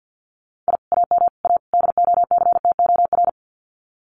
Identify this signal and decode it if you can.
Morse code, heard as a keyed tone, about 50 wpm, 700 Hz. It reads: I AM A DOCTOR
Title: Common QSO Phrases Encore 50wpm